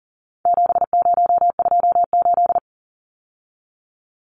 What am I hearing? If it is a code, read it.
Morse code, heard as a keyed tone, about 40 wpm, 700 Hz. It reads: 7028
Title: Numbers - 4-Digit 40wpm